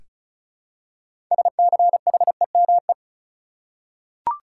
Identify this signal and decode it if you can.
Morse code, heard as a keyed tone, about 36 wpm, 700 Hz. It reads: SCHEME
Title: Six Letter Words - 35wpm